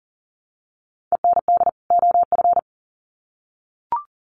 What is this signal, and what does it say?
Morse code, heard as a keyed tone, about 40 wpm, 700 Hz. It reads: END OF
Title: Sets of 2 Words 40wpm